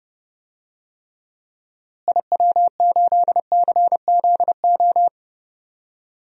Morse code, heard as a keyed tone, about 30 wpm, 700 Hz. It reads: IW8CZO
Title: DXCC-100 Call Signs - 6 Characters - Repeated 2x - 30wpm